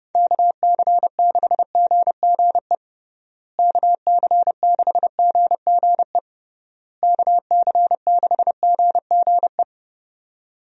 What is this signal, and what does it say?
Morse code, heard as a keyed tone, about 30 wpm, 700 Hz. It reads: KC6GGE KC6GGE KC6GGE
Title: USA Call Signs - 6 Characters - Repeated 3x - 30wpm